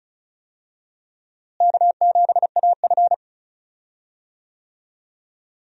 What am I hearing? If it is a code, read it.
Morse code, heard as a keyed tone, about 34 wpm, 700 Hz. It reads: K7AF
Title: USA Call Signs - 4 Characters - Repeated 3x - 35wpm